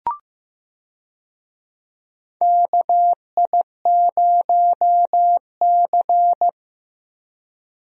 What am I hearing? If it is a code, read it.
Morse code, heard as a keyed tone, about 15 wpm, 700 Hz. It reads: KI0C
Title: USA Call Signs - 4 Characters - Repeated 2x - 15wpm